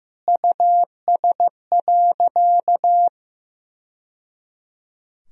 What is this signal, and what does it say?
Morse code, heard as a keyed tone, about 15 wpm, 700 Hz. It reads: US.
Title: Sentences from Top 1000 Words - Review Part 2 - 15wpm